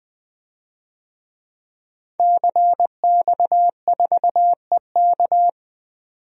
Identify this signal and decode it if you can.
Morse code, heard as a keyed tone, about 21 wpm, 700 Hz. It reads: CX4EK
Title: DXCC-100 Call Signs - 5 Characters - 20wpm